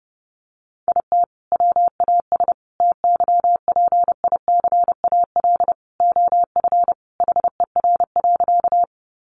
Morse code, heard as a keyed tone, about 30 wpm, 700 Hz. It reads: IT WAS TYPICAL OF HER.
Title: Sentences from Top 1100 Words 30wpm